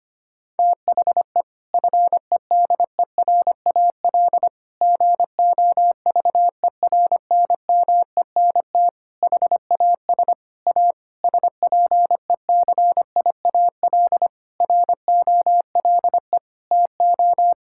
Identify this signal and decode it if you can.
Morse code, heard as a keyed tone, about 25 wpm, 700 Hz. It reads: THE FEDERAL GOVERNMENT HAS A SPECIAL ROLE TO
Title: Sentences from Top 1000 Words - Review Part 6 - 25wpm